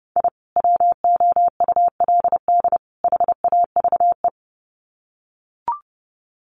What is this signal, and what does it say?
Morse code, heard as a keyed tone, about 30 wpm, 700 Hz. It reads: I WOULD HAVE
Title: Sets of 3 Words 30wpm